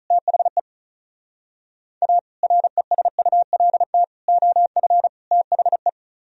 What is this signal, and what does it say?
Morse code, heard as a keyed tone, about 44 wpm, 700 Hz. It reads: THE A RESULT OF THE
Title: Speed-Racing - Sets of 4 Words 35wpm